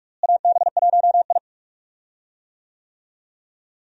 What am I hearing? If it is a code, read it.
Morse code, heard as a keyed tone, about 46 wpm, 700 Hz. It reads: AB1I
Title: USA Call Signs - 4 Characters - Repeated 3x - 45wpm